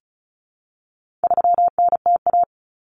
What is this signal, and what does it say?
Morse code, heard as a keyed tone, about 34 wpm, 700 Hz. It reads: 3NTA
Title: Common Cut Numbers - 4-Digit 35wpm